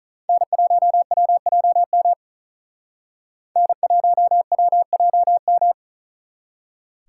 Morse code, heard as a keyed tone, about 46 wpm, 700 Hz. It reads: N1WJM N1WJM